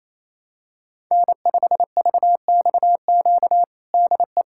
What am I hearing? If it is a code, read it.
Morse code, heard as a keyed tone, about 27 wpm, 700 Hz. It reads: N5VXQ DE
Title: USA Callsign DE Callsign 28wpm